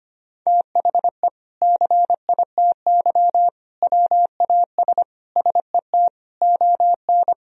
Morse code, heard as a keyed tone, about 25 wpm, 700 Hz. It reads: THE CITY WAS SET ON
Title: Sentences from Top 1000 Words - Review Part 1 - 25wpm